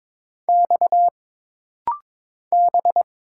Morse code, heard as a keyed tone, about 22 wpm, 700 Hz. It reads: X B
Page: This rapid-fire timing is geared to push your ICR (Instant Character Recognition) skills.